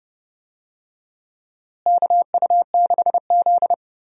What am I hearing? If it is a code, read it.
Morse code, heard as a keyed tone, about 30 wpm, 700 Hz. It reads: KU6Z
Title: USA Call Signs - 4 Characters - Repeated 3x - 30wpm